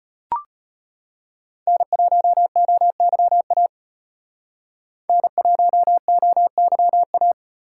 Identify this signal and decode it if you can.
Morse code, heard as a keyed tone, about 34 wpm, 700 Hz. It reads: N1OYA N1OYA